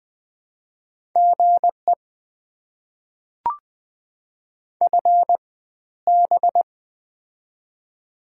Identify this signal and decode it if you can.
Morse code, heard as a keyed tone, about 21 wpm, spaced as for 13 wpm, 690 Hz. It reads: GE FB
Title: Top 20 QSO Elements - 4x Character Spacing - 20wpm